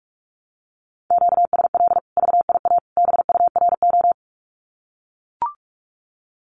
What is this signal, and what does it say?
Morse code, heard as a keyed tone, about 44 wpm, 700 Hz. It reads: QSL VIA BURO
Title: Common QSO Phrases 45wpm